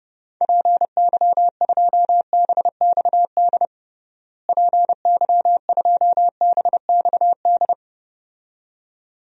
Morse code, heard as a keyed tone, about 30 wpm, 700 Hz. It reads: PY2BXD PY2BXD
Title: DXCC-100 Call Signs - 6 Characters - Repeated 2x - 30wpm